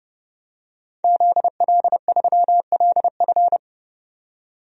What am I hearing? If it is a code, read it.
Morse code, heard as a keyed tone, about 30 wpm, 700 Hz. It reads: ZL3LF
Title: DXCC-100 Call Signs - 5 Characters - 30wpm